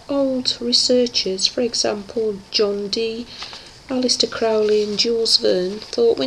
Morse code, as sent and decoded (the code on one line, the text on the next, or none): none